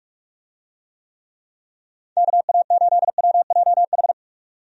Morse code, heard as a keyed tone, about 46 wpm, 700 Hz. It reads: KA8WJH
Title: USA Call Signs - 6 Characters - 45wpm